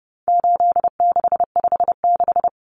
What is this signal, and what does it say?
Morse code, heard as a keyed tone, about 30 wpm, 700 Hz. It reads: 8656
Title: Common Cut Numbers - 4-Digit 30wpm